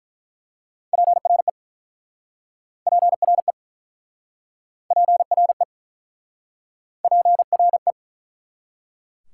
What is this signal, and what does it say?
Morse code, heard as a keyed tone, about 48 wpm, 700 Hz. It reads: PRE PRE PRE PRE